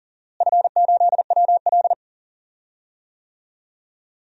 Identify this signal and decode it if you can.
Morse code, heard as a keyed tone, about 40 wpm, 700 Hz. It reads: F8WL
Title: DXCC-100 Call Signs - 4 Characters - Repeated 2x - 40wpm